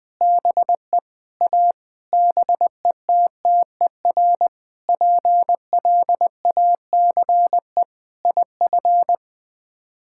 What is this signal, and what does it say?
Morse code, heard as a keyed tone, about 20 wpm, 700 Hz. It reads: BE A BETTER PLACE IF
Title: Sets of 8 Words - 20wpm